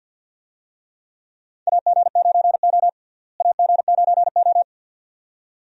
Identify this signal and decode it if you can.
Morse code, heard as a keyed tone, about 50 wpm, 700 Hz. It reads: AG9O AG9O
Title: USA Call Signs - 4 Characters - Repeated 2x - 50wpm